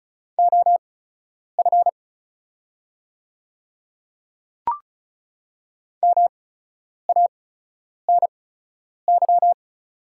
Morse code, heard as a keyed tone, about 36 wpm, spaced as for 10 wpm, 700 Hz. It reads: OF MANY
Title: 8x Character Spacing - Top 100 Words - 35wpm